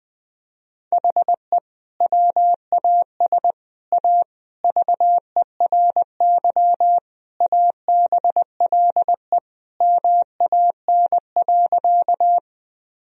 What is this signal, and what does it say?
Morse code, heard as a keyed tone, about 21 wpm, 700 Hz. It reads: HE WAS A VERY ABLE MAN.
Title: Sentences from Top 1000 Words - Review Part 2 - 20wpm